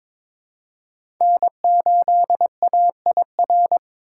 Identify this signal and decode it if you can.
Morse code, heard as a keyed tone, about 23 wpm, 700 Hz. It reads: N8AIR
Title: USA Call Signs - 5 Characters - 2x Character Spacing - 22wpm